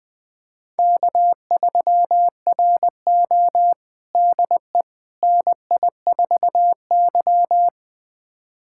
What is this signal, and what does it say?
Morse code, heard as a keyed tone, about 20 wpm, 700 Hz. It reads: K3RO DE NI4Y
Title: USA Callsign DE Callsign - 4 Characters - 20wpm